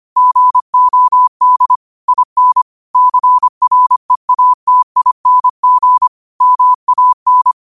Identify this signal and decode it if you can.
Morse code, heard as a keyed tone, about 25 wpm, 1000 Hz. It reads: GOD IN CREATING MAN